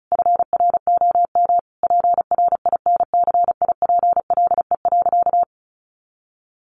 Morse code, heard as a keyed tone, about 34 wpm, 700 Hz. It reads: FROM PRINCIPLE.
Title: Sentences from Top 1400 Words 35wpm